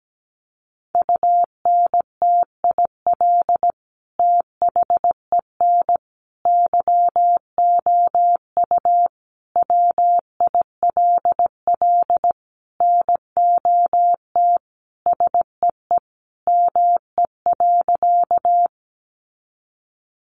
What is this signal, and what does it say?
Morse code, heard as a keyed tone, about 17 wpm, 700 Hz. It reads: UNTIL THEN YOU WILL NOT SEE ME.
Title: Sentences from Top 600 Words 17wpm